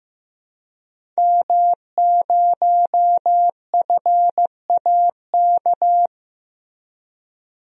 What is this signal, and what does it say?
Morse code, heard as a keyed tone, about 15 wpm, 700 Hz. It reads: M0FAK
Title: DXCC-100 Call Signs - 5 Characters - Repeated 3x - 15wpm